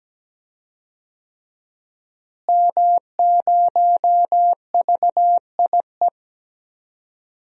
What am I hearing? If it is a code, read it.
Morse code, heard as a keyed tone, about 17 wpm, 700 Hz. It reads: M0VIE
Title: DXCC-100 Call Signs - 5 Characters - Repeated 2x - 17wpm